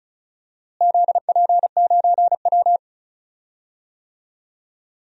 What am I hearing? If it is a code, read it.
Morse code, heard as a keyed tone, about 36 wpm, 700 Hz. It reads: ZP9W
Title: DXCC-100 Call Signs - 4 Characters - Repeated 3x - 35wpm